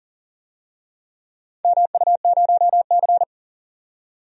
Morse code, heard as a keyed tone, about 37 wpm, 700 Hz. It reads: MU0C
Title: DXCC-100 Call Signs - 4 Characters - 40wpm